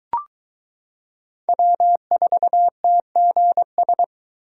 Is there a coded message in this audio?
Yes: Morse code from 1.5 to 4.0 s, heard as a keyed tone, about 22 wpm, 700 Hz. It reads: W4TGS